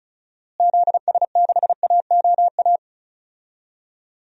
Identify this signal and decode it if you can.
Morse code, heard as a keyed tone, about 36 wpm, 700 Hz. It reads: ZS6AOA
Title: DXCC-100 Call Signs - 6 Characters - Repeated 3x - 35wpm